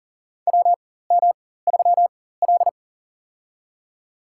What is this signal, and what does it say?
Morse code, heard as a keyed tone, about 40 wpm, spaced as for 19 wpm, 700 Hz. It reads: WM3L
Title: USA Call Signs - 4 Characters - 4x Character Spacing - 40wpm